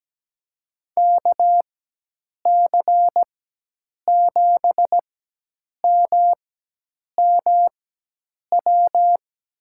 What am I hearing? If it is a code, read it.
Morse code, heard as a keyed tone, about 17 wpm, spaced as for 8 wpm, 700 Hz. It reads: KC7MMW
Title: USA Call Signs - 6 Characters - 4x Character Spacing - 17wpm